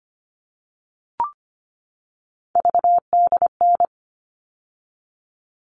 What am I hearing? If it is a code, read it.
Morse code, heard as a keyed tone, about 25 wpm, 700 Hz. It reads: VDN
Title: All Cut Numbers - 3-Digit 25wpm